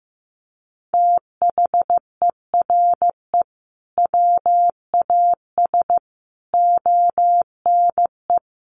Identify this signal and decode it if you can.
Morse code, heard as a keyed tone, about 15 wpm, 700 Hz. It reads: THERE WAS ONE
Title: Sentences from Top 1500 Words 15wpm